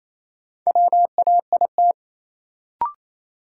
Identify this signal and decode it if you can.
Morse code, heard as a keyed tone, about 28 wpm, 700 Hz. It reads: WAIT
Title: Sentences from Top 500 Words - New Word Drill - 28wpm